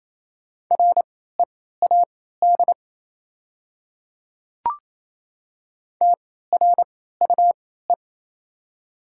Morse code, heard as a keyed tone, about 28 wpm, spaced as for 16 wpm, 690 Hz. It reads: READ TRUE
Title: 3x Character Spacing - Top 500 Words - 28wpm